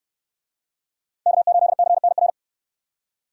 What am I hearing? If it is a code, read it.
Morse code, heard as a keyed tone, about 67 wpm, 700 Hz. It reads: K8CNR